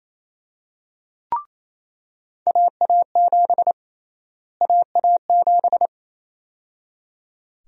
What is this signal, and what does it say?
Morse code, heard as a keyed tone, about 28 wpm, 700 Hz. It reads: AA7 AA7
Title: USA Call Sign Prefixes - Repeated 2x - 28wpm